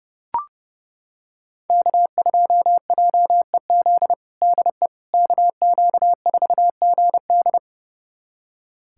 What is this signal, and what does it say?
Morse code, heard as a keyed tone, about 30 wpm, 700 Hz. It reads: K2JEZ DE KQ4GD
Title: USA Callsign DE Callsign - 5 Characters - 30wpm